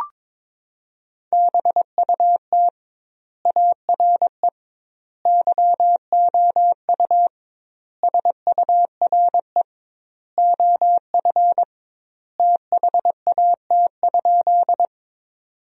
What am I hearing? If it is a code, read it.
Morse code, heard as a keyed tone, about 22 wpm, 700 Hz. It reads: BUT ARE YOU SURE OF THAT?
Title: 2x Word-Spacing - Sentences from Top 300 Words - 22wpm